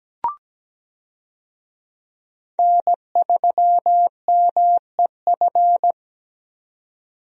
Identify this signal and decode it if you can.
Morse code, heard as a keyed tone, about 17 wpm, 700 Hz. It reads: N3MEF
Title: USA Call Signs - 5 Characters - 17wpm